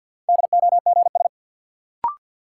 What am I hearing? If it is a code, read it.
Morse code, heard as a keyed tone, about 49 wpm, 700 Hz. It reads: DOGS
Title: Sentences from Top 1600 Words - New Word Drill - 50wpm